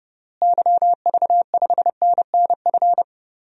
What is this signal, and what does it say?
Morse code, heard as a keyed tone, about 31 wpm, 700 Hz. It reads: YV5NNF
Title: DXCC-100 Call Signs - 6 Characters - Repeated 3x - 30wpm